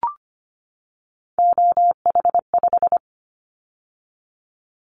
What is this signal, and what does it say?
Morse code, heard as a keyed tone, about 25 wpm, 700 Hz. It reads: OH5
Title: Callsign Prefixes for Top 50 DXCC - 25wpm